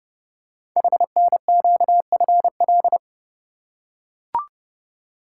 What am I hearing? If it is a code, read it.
Morse code, heard as a keyed tone, about 30 wpm, 700 Hz. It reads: HNQFL
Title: Code Groups - 5 Letters - 30wpm